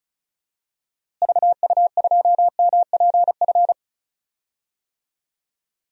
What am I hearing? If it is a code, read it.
Morse code, heard as a keyed tone, about 34 wpm, 700 Hz. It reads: VU2MPF
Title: DXCC-100 Call Signs - 6 Characters - Repeated 2x - 35wpm